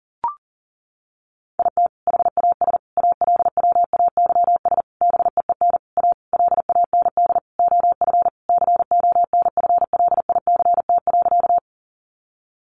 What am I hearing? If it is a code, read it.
Morse code, heard as a keyed tone, about 40 wpm, 700 Hz. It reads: IT HAS ALWAYS BEEN A LAND OF CONFLICT.
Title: Sentences from Top 1300 Words 40wpm